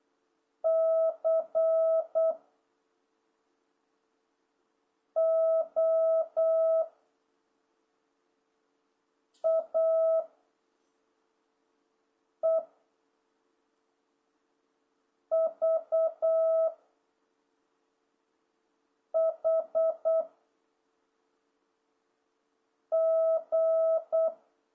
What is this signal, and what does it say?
Morse code, heard as a keyed tone, about 8 wpm, 660 Hz. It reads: C O A E V H G